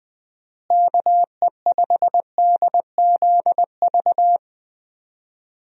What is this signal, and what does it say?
Morse code, heard as a keyed tone, about 20 wpm, 700 Hz. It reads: KE5DZV
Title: USA Call Signs - 6 Characters - Repeated 2x - 20wpm